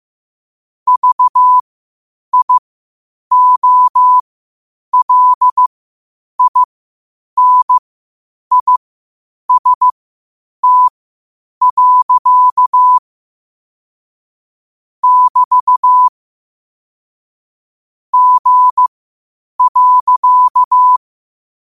Morse code, heard as a keyed tone, about 15 wpm, spaced as for 8 wpm, 1000 Hz. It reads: VIOLINIST. <BT> G.
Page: Quotes for Sat, 16 Aug 2025 in Morse Code at 8 words per minute.